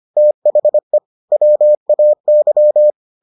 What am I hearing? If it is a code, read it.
Morse code, heard as a keyed tone, about 25 wpm, 600 Hz. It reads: THE WAY